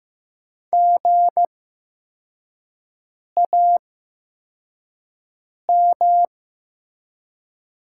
Morse code, heard as a keyed tone, about 15 wpm, spaced as for 4 wpm, 700 Hz. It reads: GAM
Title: 8x Character Spacing - Top 500 Words - 15wpm